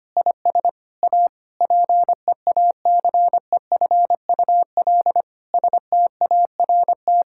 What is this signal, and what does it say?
Morse code, heard as a keyed tone, about 25 wpm, 700 Hz. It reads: IS A PEACEFUL START
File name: practice / 25wpm / Difficult Sentences with Hints - 25wpm